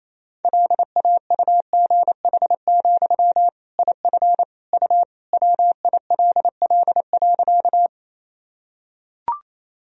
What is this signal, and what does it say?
Morse code, heard as a keyed tone, about 28 wpm, 700 Hz. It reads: LAUGH, IF U WILL.
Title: Sentences from Top 500 Words and Abbreviations 28wpm